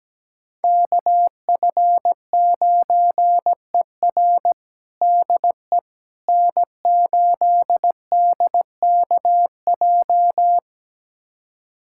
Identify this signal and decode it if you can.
Morse code, heard as a keyed tone, about 17 wpm, 700 Hz. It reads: KF9ER DE N8DKJ
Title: USA Callsign DE Callsign 17wpm